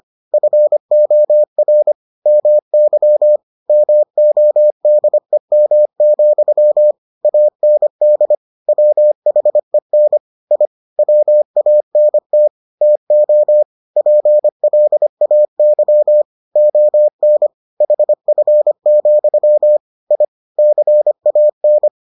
Morse code, heard as a keyed tone, about 25 wpm, 600 Hz. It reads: FOR MY MODEM, AND WHEN I WANT TO PLAY ON HF, I CAN